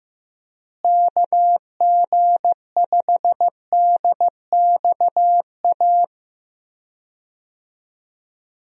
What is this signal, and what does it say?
Morse code, heard as a keyed tone, about 16 wpm, 700 Hz. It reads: KG5DXA
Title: USA Call Signs - 6 Characters - Repeated 2x - 15wpm